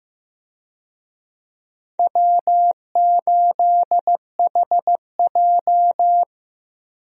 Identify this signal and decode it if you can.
Morse code, heard as a keyed tone, about 15 wpm, 700 Hz. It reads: W8HJ
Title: USA Call Signs - 4 Characters - Repeated 2x - 15wpm